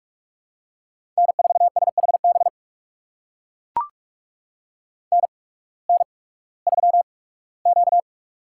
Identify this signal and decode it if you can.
Morse code, heard as a keyed tone, about 44 wpm, spaced as for 21 wpm, 700 Hz. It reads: N4SHB NN3Q
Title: USA Call Signs - 8x Character Spacing - 45wpm